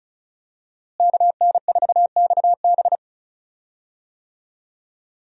Morse code, heard as a keyed tone, about 34 wpm, 700 Hz. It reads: KN4XB
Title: USA Call Signs - 5 Characters - Repeated 3x - 35wpm